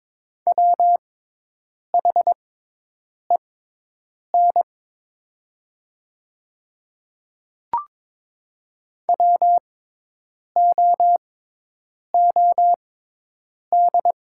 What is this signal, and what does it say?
Morse code, heard as a keyed tone, about 22 wpm, spaced as for 8 wpm, 700 Hz. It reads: WHEN WOOD
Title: 6x Character Spacing - Top 500 Words - 22wpm